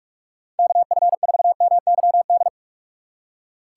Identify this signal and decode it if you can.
Morse code, heard as a keyed tone, about 47 wpm, 700 Hz. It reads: KF4MYD